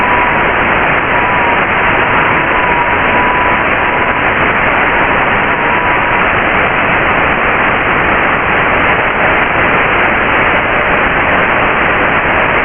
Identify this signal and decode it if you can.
FT8.
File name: Radio / SDR Audio / audio z 19-48-00 11-09-2022